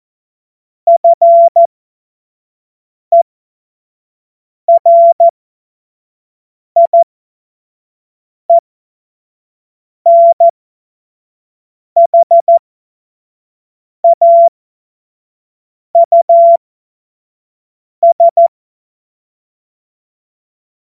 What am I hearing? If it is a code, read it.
Morse code, heard as a keyed tone, about 14 wpm, spaced as for 5 wpm, 680 Hz. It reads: FERIENHAUS
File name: Morsekurs / Witze / Witz 12 März 2024 25 F70